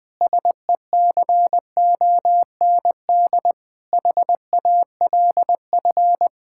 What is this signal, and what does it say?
Morse code, heard as a keyed tone, about 20 wpm, 700 Hz. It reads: SECOND HALF
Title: Sentences with 5-Word Phrases - Spoken Beginning - 20wpm